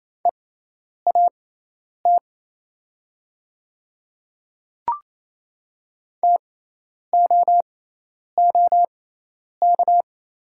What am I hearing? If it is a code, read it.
Morse code, heard as a keyed tone, about 28 wpm, spaced as for 10 wpm, 700 Hz. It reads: EAT TOOK
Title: 6x Character Spacing - Top 500 Words - 28wpm